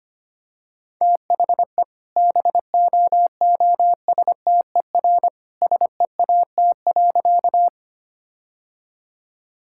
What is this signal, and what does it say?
Morse code, heard as a keyed tone, about 25 wpm, 700 Hz. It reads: THE BOOSTER SEAT.
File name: practice / 25wpm / All-Age Appropriate Jokes - 25wpm